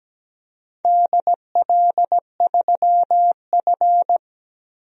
Morse code, heard as a keyed tone, about 17 wpm, 700 Hz. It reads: DL3F
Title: DXCC-100 Call Signs - 4 Characters - Repeated 2x - 17wpm